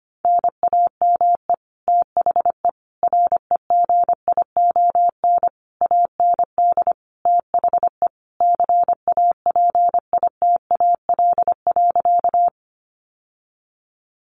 Morse code, heard as a keyed tone, about 25 wpm, 700 Hz. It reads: NAME THE REGION AND THE CAPITAL.
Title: Sentences from Top 1500 Words 25wpm